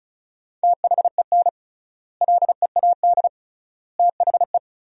Morse code, heard as a keyed tone, about 36 wpm, 700 Hz. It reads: THEN LEAD THE
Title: 3x Word-Spacing - Sentences from Top 500 Words - 35wpm